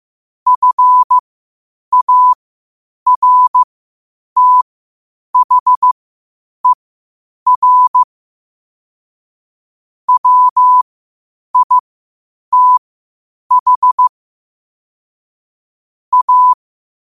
Morse code, heard as a keyed tone, about 15 wpm, spaced as for 8 wpm, 1000 Hz. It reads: FARTHER WITH A
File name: qotdfiles / 8wpm